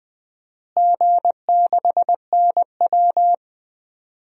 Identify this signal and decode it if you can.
Morse code, heard as a keyed tone, about 20 wpm, 700 Hz. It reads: G6NW
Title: DXCC-100 Call Signs - 4 Characters - Repeated 2x - 20wpm